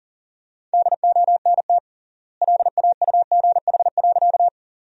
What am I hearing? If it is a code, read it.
Morse code, heard as a keyed tone, about 40 wpm, 700 Hz. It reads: DONT LAUGH.
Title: 3x Word-Spacing - Sentences from Top 500 Words - 40wpm